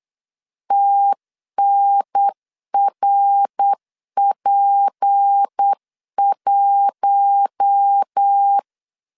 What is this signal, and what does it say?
Morse code, heard as a keyed tone, about 8 wpm, 800 Hz. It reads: TNRP1
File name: LEZIONI / 9 28 8wpm 3512